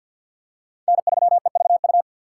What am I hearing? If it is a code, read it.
Morse code, heard as a keyed tone, about 52 wpm, 700 Hz. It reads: N3EVU
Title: USA Call Signs - 5 Characters - 6x Character Spacing - 50wpm